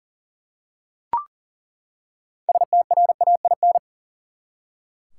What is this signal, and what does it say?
Morse code, heard as a keyed tone, about 39 wpm, 700 Hz. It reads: STRAIN
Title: Six Letter Words - 40wpm